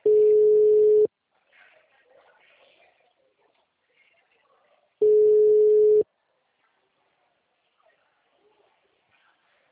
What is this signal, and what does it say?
Ringback tone (Europe, 425 Hz).
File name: out- -2008- - - .68203